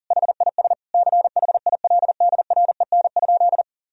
Morse code, heard as a keyed tone, about 40 wpm, 700 Hz. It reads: HIS CHILDREN?
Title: Sentences from Top 1000 Words - Review Part 3 - 40wpm